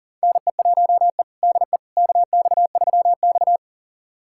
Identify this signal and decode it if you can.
Morse code, heard as a keyed tone, about 40 wpm, 700 Hz. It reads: NE1E DE KX3X
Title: USA Callsign DE Callsign - 4 Characters - 40wpm